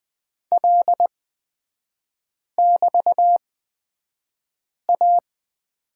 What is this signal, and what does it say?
Morse code, heard as a keyed tone, about 20 wpm, spaced as for 5 wpm, 700 Hz. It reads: L<BT>A
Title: 224 Lesson 36e - BT - Three Characters 20wpm